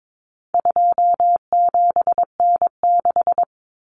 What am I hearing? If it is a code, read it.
Morse code, heard as a keyed tone, about 22 wpm, 700 Hz. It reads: 27N6
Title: Common Cut Numbers - 4-Digit 22wpm